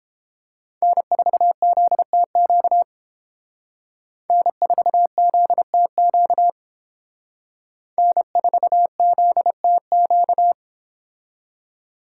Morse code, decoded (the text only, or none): N4ZTQ N4ZTQ N4ZTQ